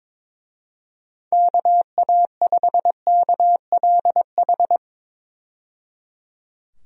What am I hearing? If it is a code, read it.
Morse code, heard as a keyed tone, about 22 wpm, 700 Hz. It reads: KA5KLH
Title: USA Call Signs - 6 Characters - 22wpm